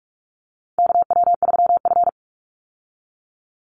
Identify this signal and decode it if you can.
Morse code, heard as a keyed tone, about 44 wpm, 700 Hz. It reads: KW3F
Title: Call Signs Repeated 2x - 45wpm